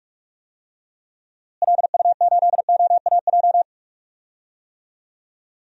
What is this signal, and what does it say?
Morse code, heard as a keyed tone, about 46 wpm, 700 Hz. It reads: LU8OAJ
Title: DXCC-100 Call Signs - 6 Characters - Repeated 2x - 45wpm